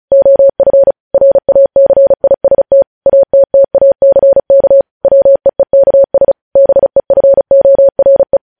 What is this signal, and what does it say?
Morse code, heard as a keyed tone, about 36 wpm, 550 Hz. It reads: OF RACIST ATTACK WEEKS BEFORE
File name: podcast / News headlines-1x-35x35